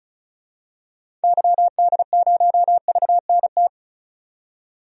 Morse code, heard as a keyed tone, about 36 wpm, 700 Hz. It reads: YD0VNT
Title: DXCC-100 Call Signs - 6 Characters - 35wpm